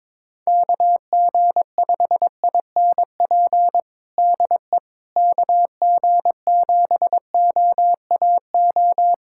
Morse code, heard as a keyed tone, about 22 wpm, 700 Hz. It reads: KG5INP DE KG7OAO
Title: USA Callsign DE Callsign - 6 Characters - 22wpm